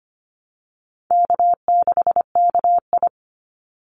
Morse code, heard as a keyed tone, about 25 wpm, 700 Hz. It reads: K6KI
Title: Call Signs - 25wpm